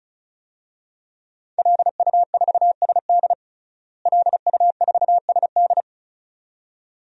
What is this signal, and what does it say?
Morse code, heard as a keyed tone, about 36 wpm, 700 Hz. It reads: LU4SD LU4SD
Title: DXCC-100 Call Signs - 5 Characters - Repeated 2x - 35wpm